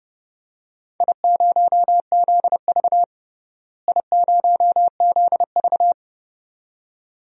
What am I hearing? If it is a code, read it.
Morse code, heard as a keyed tone, about 30 wpm, 700 Hz. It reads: I0ZV I0ZV
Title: DXCC-100 Call Signs - 4 Characters - Repeated 2x - 30wpm